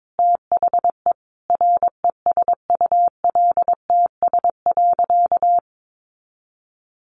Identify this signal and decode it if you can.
Morse code, heard as a keyed tone, about 22 wpm, 700 Hz. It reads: THE RESULTS.
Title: A Sound Of Thunder 22wpm